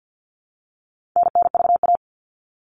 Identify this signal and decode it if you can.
Morse code, heard as a keyed tone, about 48 wpm, 700 Hz. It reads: NNVA
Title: All Cut Numbers - 4-Digit 50wpm